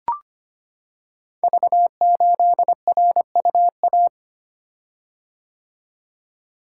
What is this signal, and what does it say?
Morse code, heard as a keyed tone, about 25 wpm, 700 Hz. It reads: V8RUA
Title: Code Groups - 5 Letter:Numbers - 25wpm